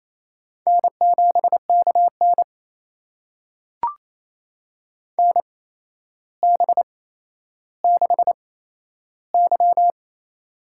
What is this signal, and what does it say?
Morse code, heard as a keyed tone, about 28 wpm, spaced as for 12 wpm, 700 Hz. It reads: N7KN NB6Y
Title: USA Call Signs - 4 Characters - 8x Character Spacing - 28wpm